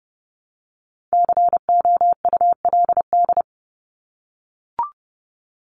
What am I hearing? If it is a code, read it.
Morse code, heard as a keyed tone, about 30 wpm, 700 Hz. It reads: COULD
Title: Top 200 Words Repeated 30wpm